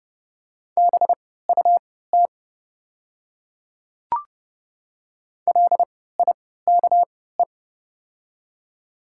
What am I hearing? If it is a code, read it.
Morse code, heard as a keyed tone, about 30 wpm, spaced as for 17 wpm, 700 Hz. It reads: BUT LIKE
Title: 3x Character Spacing - Top 100 Words - 30wpm